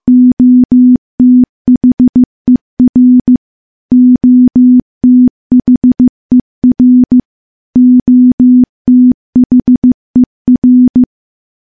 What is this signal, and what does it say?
Morse code, heard as a keyed tone, about 15 wpm, 250 Hz. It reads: OTHER OTHER OTHER
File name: morse / morse-training-15WPM